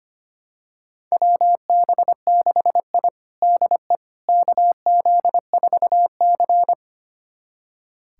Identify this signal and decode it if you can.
Morse code, heard as a keyed tone, about 25 wpm, 700 Hz. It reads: WB6I DE KZ4C
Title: USA Callsign DE Callsign - 4 Characters - 25wpm